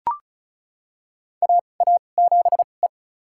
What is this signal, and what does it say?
Morse code, heard as a keyed tone, about 36 wpm, spaced as for 25 wpm, 700 Hz. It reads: AA7E
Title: USA Call Signs - 4 Characters - 2x Character Spacing - 35wpm